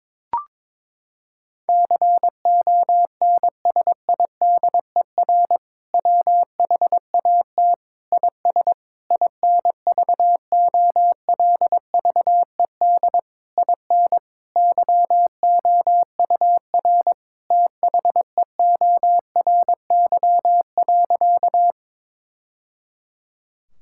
Morse code, heard as a keyed tone, about 22 wpm, 700 Hz. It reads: CONSIDER WHAT IS INVOLVED IN YOUR THEORY.
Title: Sentences from Top 1000 Words - Review Part 5 - 22wpm